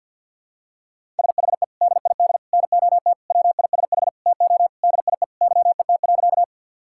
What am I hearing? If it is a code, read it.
Morse code, heard as a keyed tone, about 52 wpm, 700 Hz. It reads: SHE DID NOT WISH TO DIE YET.
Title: Sentences from Top 1600 Words 50wpm